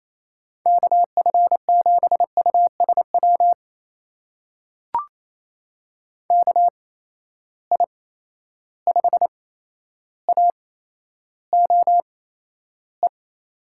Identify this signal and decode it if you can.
Morse code, heard as a keyed tone, about 28 wpm, spaced as for 12 wpm, 700 Hz. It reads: KF7USW KI5AOE
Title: USA Call Signs - 6 Characters - 8x Character Spacing - 28wpm